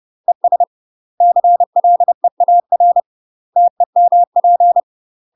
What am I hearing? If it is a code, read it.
Morse code, heard as a keyed tone, about 30 wpm, 700 Hz. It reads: ES CLEAR TEMP
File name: UK MASTER QSOs TRAINING / 30WPM / UK MASTER QSOs 0009